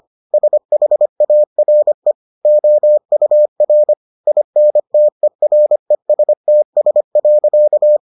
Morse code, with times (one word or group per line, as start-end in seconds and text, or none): 0.3-2.1 SHARE
2.5-3.9 OUR
4.3-8.0 INTERESTS.